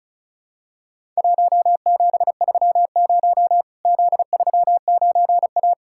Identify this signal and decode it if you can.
Morse code, heard as a keyed tone, about 36 wpm, 700 Hz. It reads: 1730 Z39A
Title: Worked All Europe DX Contest - QTC Exchange - 35wpm